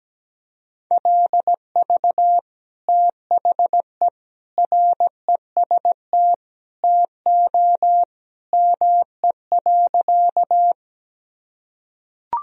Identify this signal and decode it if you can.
Morse code, heard as a keyed tone, about 17 wpm, 700 Hz. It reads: LV THE REST TO ME.
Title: Sentences from Top 500 Words and Abbreviations 17wpm